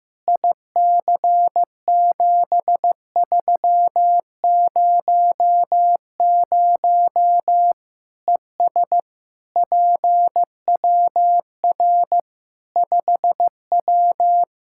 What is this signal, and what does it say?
Morse code, heard as a keyed tone, about 15 wpm, 700 Hz. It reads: IC7300 ES PWR 5W
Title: Practice QSO - Exchange 2 - Info - 15wpm